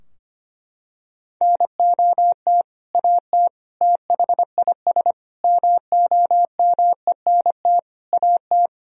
Morse code, heard as a keyed tone, about 25 wpm, 700 Hz. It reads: NOT AT THIS MOMENT AT
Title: Sentences from Top 1000 Words - Review Part 4 - 25wpm